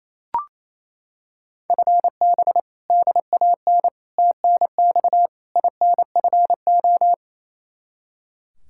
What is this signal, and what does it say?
Morse code, heard as a keyed tone, about 28 wpm, 700 Hz. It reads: FB DAN TNX INFO
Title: Practice QSO - Exchange 3 - More Info - 28wpm